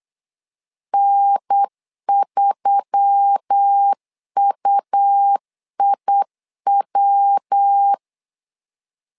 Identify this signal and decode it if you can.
Morse code, heard as a keyed tone, about 8 wpm, 800 Hz. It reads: N3UIW
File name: LEZIONI / 4 Q1S3 8wpm 3037